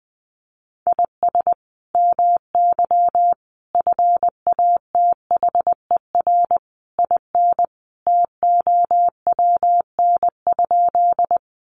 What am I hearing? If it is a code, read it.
Morse code, heard as a keyed tone, about 20 wpm, 700 Hz. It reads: IS MY FATHER IN TOWN?
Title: Sentences from Top 500 Words 20wpm